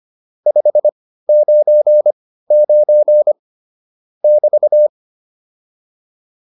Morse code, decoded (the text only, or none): 599 <BT>